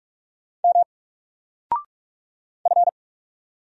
Morse code, heard as a keyed tone, about 44 wpm, 700 Hz. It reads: M F
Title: Single Letter-Number 45wpm